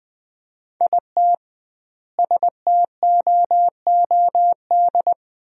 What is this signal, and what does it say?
Morse code, heard as a keyed tone, about 19 wpm, 700 Hz. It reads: IT STOOD
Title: 2x Word-Spacing - Sentences from Top 500 Words - 20wpm